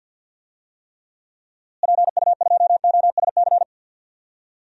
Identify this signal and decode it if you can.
Morse code, heard as a keyed tone, about 52 wpm, 700 Hz. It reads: PU2OSC
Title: DXCC-100 Call Signs - 6 Characters - Repeated 2x - 50wpm